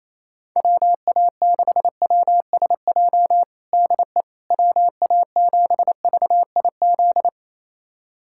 Morse code, heard as a keyed tone, about 28 wpm, 700 Hz. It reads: WA6WSJ DE WA7VIZ
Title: USA Callsign DE Callsign - 6 Characters - 28wpm